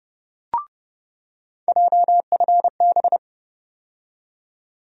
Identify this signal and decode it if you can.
Morse code, heard as a keyed tone, about 32 wpm, 700 Hz. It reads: JFB
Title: USA Call Sign Postfixes - 3 Letters 30wpm